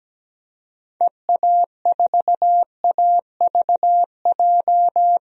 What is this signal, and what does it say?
Morse code, heard as a keyed tone, about 17 wpm, 700 Hz. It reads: EA4AVJ
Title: DXCC-100 Call Signs - 6 Characters - 17wpm